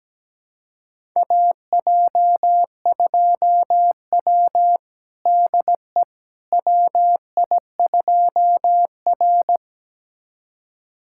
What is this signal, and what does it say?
Morse code, heard as a keyed tone, about 17 wpm, 700 Hz. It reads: AJ2W DE WI2R
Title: USA Callsign DE Callsign - 4 Characters - 17wpm